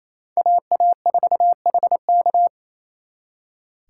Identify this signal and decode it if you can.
Morse code, heard as a keyed tone, about 32 wpm, 700 Hz. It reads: AA4HK
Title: Speed Racing - USA Call Signs 28wpm